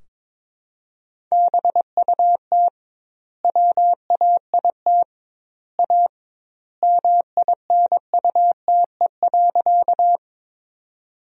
Morse code, heard as a keyed tone, about 22 wpm, 700 Hz. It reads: BUT WAIT A MINUTE.
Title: 2x Word-Spacing - Sentences from Top 500 Words - 22wpm